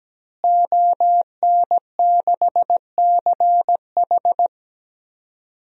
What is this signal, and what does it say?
Morse code, heard as a keyed tone, about 17 wpm, 700 Hz. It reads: ON6CH
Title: DXCC-100 Call Signs - 5 Characters - Repeated 3x - 17wpm